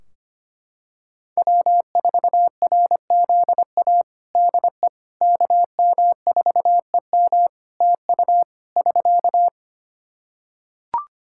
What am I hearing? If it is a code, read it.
Morse code, heard as a keyed tone, about 25 wpm, 700 Hz. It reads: W4RZA DE KM4EM TU <SK>
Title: Practice QSO - Exchange 4 - Ending - 25wpm